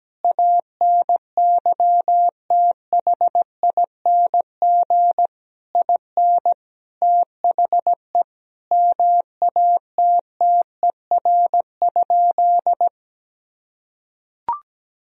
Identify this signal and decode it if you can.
Morse code, heard as a keyed tone, about 17 wpm, 700 Hz. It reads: ANYTHING IN THE MATTER?
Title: Sentences from Top 1000 Words - Review Part 4 - 17wpm